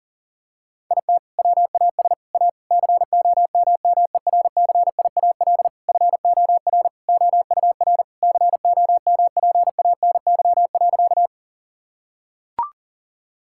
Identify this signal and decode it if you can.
Morse code, heard as a keyed tone, about 40 wpm, 700 Hz. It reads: IT WAS A COMMERCIAL FOR OUR COMPANY.
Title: Sentences from Top 1000 Words - Review Part 5 - 40wpm